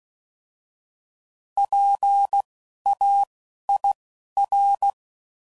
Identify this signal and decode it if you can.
Morse code, heard as a keyed tone, about 16 wpm, spaced as for 11 wpm, 800 Hz. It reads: PAIR
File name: Prefixes / By First Letter / Ps